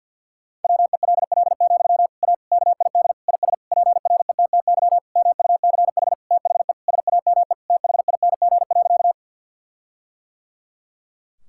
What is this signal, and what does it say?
Morse code, heard as a keyed tone, about 51 wpm, 700 Hz. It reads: WELL, A KID IS PRETTY MUCH THE SAME THING.
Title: Sentences from Top 1600 Words 50wpm